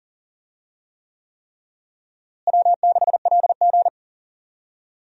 Morse code, heard as a keyed tone, about 40 wpm, 700 Hz. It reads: W6LG
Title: USA Call Signs - 4 Characters - Repeated 3x - 40wpm